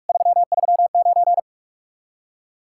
Morse code, heard as a keyed tone, about 49 wpm, 700 Hz. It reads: 339